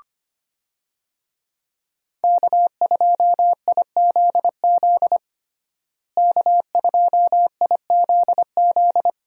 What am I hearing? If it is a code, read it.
Morse code, heard as a keyed tone, about 25 wpm, 700 Hz. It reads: K2IZZ K2IZZ
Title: USA Call Signs - 5 Characters - Repeated 2x - 25wpm